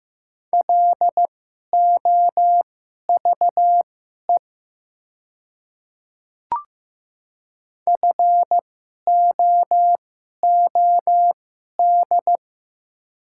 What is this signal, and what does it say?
Morse code, heard as a keyed tone, about 15 wpm, spaced as for 11 wpm, 700 Hz. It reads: LOVE FOOD
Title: 2x Character Spacing - Top 500 Words - 15wpm